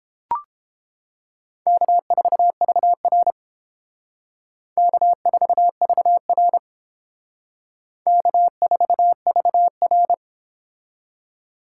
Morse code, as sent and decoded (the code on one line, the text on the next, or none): -.- ....- ...- .-. / -.- ....- ...- .-. / -.- ....- ...- .-.
K4VR K4VR K4VR